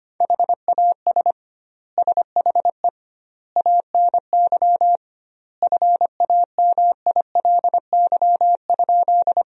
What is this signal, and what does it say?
Morse code, heard as a keyed tone, about 25 wpm, 700 Hz. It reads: HAS SHE ANY FAMILY?
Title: 2x Word-Spacing - Sentences from Top 400 Words - 25wpm